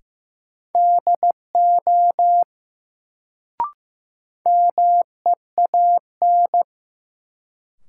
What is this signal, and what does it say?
Morse code, heard as a keyed tone, about 15 wpm, 700 Hz. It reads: DO MEAN
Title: Sets of 2 Words - New Word Drill - 15wpm